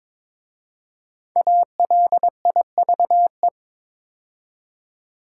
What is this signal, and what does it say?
Morse code, heard as a keyed tone, about 22 wpm, 690 Hz. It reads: ALIVE
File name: 5 Letter Words - First 3 Letters Given - 22wpm